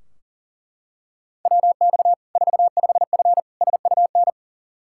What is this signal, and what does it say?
Morse code, heard as a keyed tone, about 40 wpm, 700 Hz. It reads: WX 45F SUN
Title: Practice QSO - Exchange 2 - Info - 40wpm